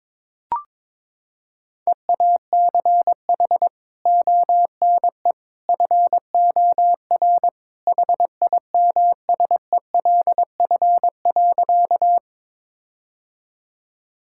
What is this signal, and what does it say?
Morse code, heard as a keyed tone, about 21 wpm, 700 Hz. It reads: EACH ONE FOR HIMSELF.
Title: Sentences from Top 1600 Words 22wpm